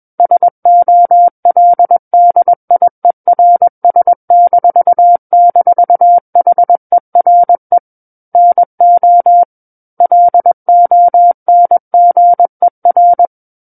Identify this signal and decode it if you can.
Morse code, heard as a keyed tone, about 21 wpm, 700 Hz. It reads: SOLDIERS--HERE NO LONGER
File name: Book One / 12-Chapter 12 - 21 WPM